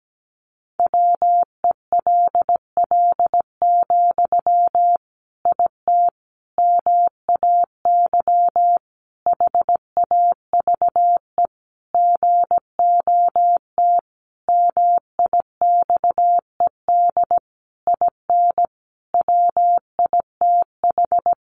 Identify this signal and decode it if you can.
Morse code, heard as a keyed tone, about 17 wpm, 700 Hz. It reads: WELL, IT MAY HAVE GOT MIXED IN WITH
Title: Sentences from Top 1500 Words 17wpm